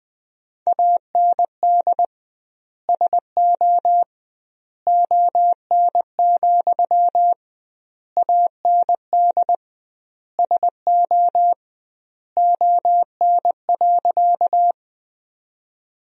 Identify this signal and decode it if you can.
Morse code, heard as a keyed tone, about 20 wpm, 700 Hz. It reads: AND SO ON, AND SO ON.
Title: 2x Word-Spacing - Sentences from Top 100 Words - 20wpm